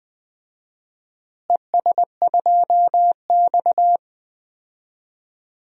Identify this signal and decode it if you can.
Morse code, heard as a keyed tone, about 20 wpm, 700 Hz. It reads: ES2X
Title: DXCC-100 Call Signs - 4 Characters - Repeated 2x - 20wpm